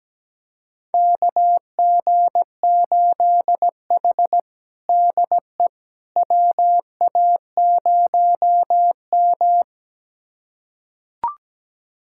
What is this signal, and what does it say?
Morse code, heard as a keyed tone, about 17 wpm, 700 Hz. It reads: KG8H DE WA0M
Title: USA Callsign DE Callsign - 4 Characters - 17wpm